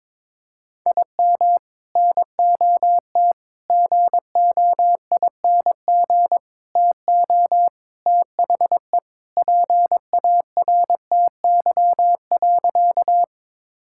Morse code, heard as a keyed tone, about 22 wpm, 700 Hz. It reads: IM NOT GOING TO THE PARTY.
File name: practice / 22wpm / Sentences with 3-Word Phrases - 22wpm